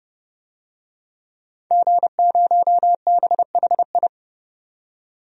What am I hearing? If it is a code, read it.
Morse code, heard as a keyed tone, about 31 wpm, 700 Hz. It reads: G0BHI
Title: DXCC-100 Call Signs - 5 Characters - 30wpm